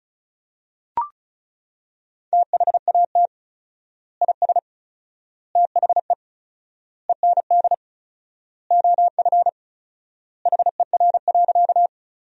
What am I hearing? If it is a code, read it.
Morse code, heard as a keyed tone, about 34 wpm, 700 Hz. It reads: THAT IS THE END OF HER.
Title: 4x Word-Spacing - Sentences from Top 200 Words - 35wpm